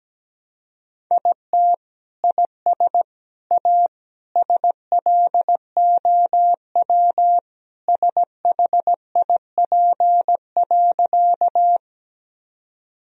Morse code, heard as a keyed tone, about 17 wpm, 700 Hz. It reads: IT IS A SLOW SHIP.
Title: Sentences from Top 1000 Words - Review Part 2 - 17wpm